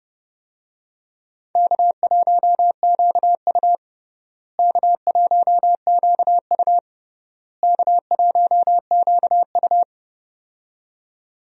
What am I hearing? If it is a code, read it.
Morse code, heard as a keyed tone, about 32 wpm, 700 Hz. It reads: K1QU K1QU K1QU
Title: USA Call Signs - 4 Characters - Repeated 3x - 30wpm